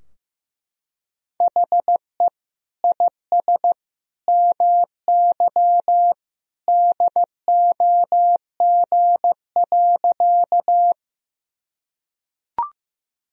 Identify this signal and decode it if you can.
Morse code, heard as a keyed tone, about 15 wpm, 700 Hz. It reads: HE IS MY DOG.
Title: Sentences from Top 1000 Words - Review Part 2 - 15wpm